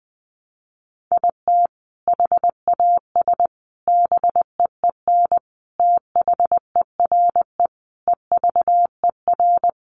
Morse code, heard as a keyed tone, about 20 wpm, 700 Hz. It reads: IT HAS BEEN THERE EVER
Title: Sentences from Top 400 Words 20wpm